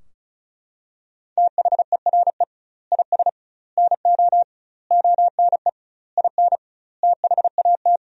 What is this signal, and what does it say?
Morse code, heard as a keyed tone, about 36 wpm, 700 Hz. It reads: THERE IS NO ONE IN THAT
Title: 2x Word-Spacing - Sentences from Top 500 Words - 35wpm